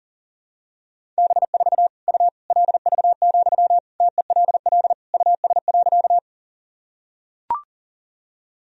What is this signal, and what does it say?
Morse code, heard as a keyed tone, about 40 wpm, 700 Hz. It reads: B4 U LV, TELL US.
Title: Sentences from Top 400 Words and Abbreviations 40wpm